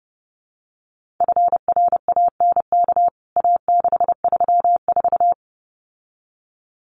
Morse code, heard as a keyed tone, about 30 wpm, 700 Hz. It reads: FRANK A634
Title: CWT Exchange 30wpm